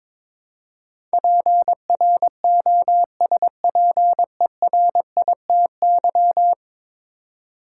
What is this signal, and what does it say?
Morse code, heard as a keyed tone, about 22 wpm, 700 Hz. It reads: PROSPERITY
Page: morse-code-ninja-practice-22wpm